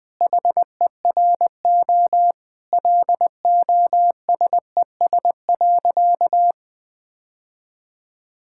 Morse code, heard as a keyed tone, about 20 wpm, 700 Hz. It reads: HERO LOSES.
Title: Sentences with 6-Word Phrases - 20wpm